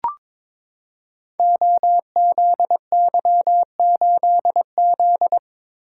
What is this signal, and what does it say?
Morse code, heard as a keyed tone, about 23 wpm, 690 Hz. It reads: OZY8Z
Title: Code Groups - 5 Letter:Numbers - 22wpm